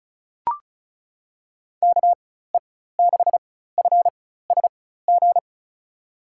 Morse code, heard as a keyed tone, about 36 wpm, spaced as for 16 wpm, 700 Hz. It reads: KE6FSG
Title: USA Call Signs - 4x Character Spacing - 35wpm